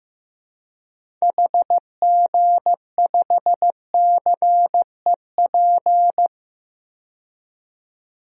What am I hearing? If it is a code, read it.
Morse code, heard as a keyed tone, about 15 wpm, 700 Hz. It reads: HG5CEP
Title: DXCC-100 Call Signs - 6 Characters - 15wpm